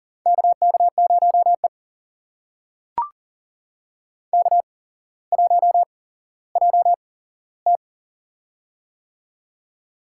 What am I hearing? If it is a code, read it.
Morse code, heard as a keyed tone, about 40 wpm, spaced as for 17 wpm, 700 Hz. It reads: KK0E K1JT
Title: USA Call Signs - 4 Characters - 8x Character Spacing - 40wpm